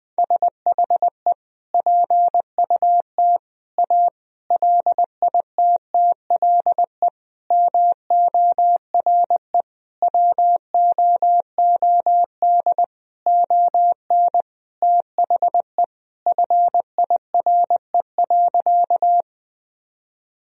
Morse code, heard as a keyed tone, about 20 wpm, 700 Hz. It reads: SHE PUT A LITTLE MORE WOOD ON THE FIRE.
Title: Sentences from Top 1000 Words - Review Part 2 - 20wpm